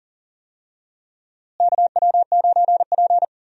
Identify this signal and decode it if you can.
Morse code, heard as a keyed tone, about 40 wpm, 700 Hz. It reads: KW9P
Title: USA Call Signs - 4 Characters - Repeated 3x - 40wpm